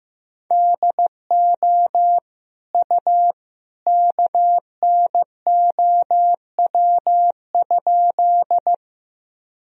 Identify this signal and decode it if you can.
Morse code, heard as a keyed tone, about 15 wpm, 700 Hz. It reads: DO U KNOW?
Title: Sentences from Top 100 Words and Abbreviations 15wpm